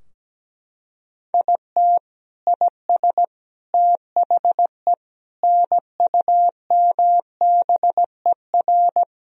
Morse code, heard as a keyed tone, about 17 wpm, 700 Hz. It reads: IT IS THE NUMBER
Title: Sentences from Top 1000 Words - Review Part 2 - 17wpm